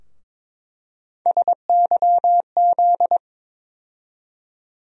Morse code, heard as a keyed tone, about 22 wpm, 700 Hz. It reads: SYZ
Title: USA Call Sign Postfixes - 3 Letters 22wpm